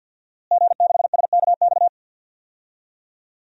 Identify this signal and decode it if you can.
Morse code, heard as a keyed tone, about 46 wpm, 700 Hz. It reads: G6SKX
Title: DXCC-100 Call Signs - 5 Characters - Repeated 2x - 50wpm